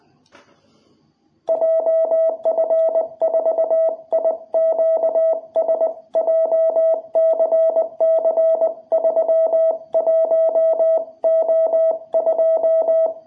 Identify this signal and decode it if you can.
Morse code, heard as a keyed tone, about 20 wpm, 660 Hz. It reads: JF4IQS JCC31O2
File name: jh8cbh / 2021cw